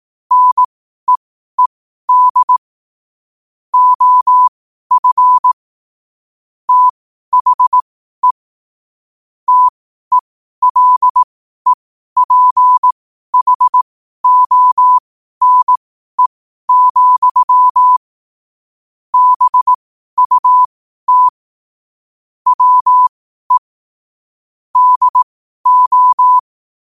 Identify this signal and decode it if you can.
Morse code, heard as a keyed tone, about 18 wpm, spaced as for 12 wpm, 1000 Hz. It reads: NEED OF THE TELEPHONE, BUT WE DO
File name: qotdfiles / 12wpm